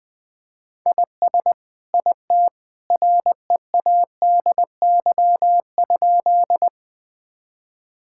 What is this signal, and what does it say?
Morse code, heard as a keyed tone, about 21 wpm, 700 Hz. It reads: IS IT READY?
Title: Sentences with 2-Word Phrases - 20wpm